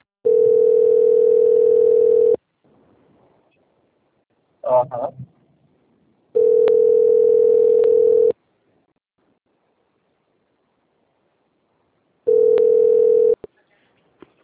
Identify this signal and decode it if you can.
Ringback tone (North America), 440 + 480 Hz.
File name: out- - - - - .32207